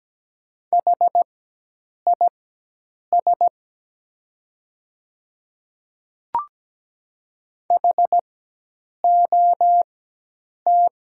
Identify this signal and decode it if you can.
Morse code, heard as a keyed tone, about 17 wpm, spaced as for 8 wpm, 700 Hz. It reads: HIS HOT
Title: 4x Character Spacing - Top 300 Words - 17wpm